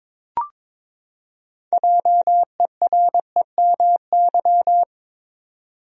Morse code, heard as a keyed tone, about 22 wpm, 700 Hz. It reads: JEREMY
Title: USA Common Names - 22wpm